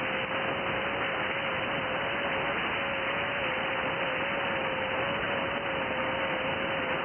FT8.
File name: IC-7100